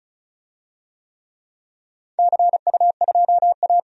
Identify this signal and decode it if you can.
Morse code, heard as a keyed tone, about 36 wpm, 700 Hz. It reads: CU2A
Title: DXCC-100 Call Signs - 4 Characters - 35wpm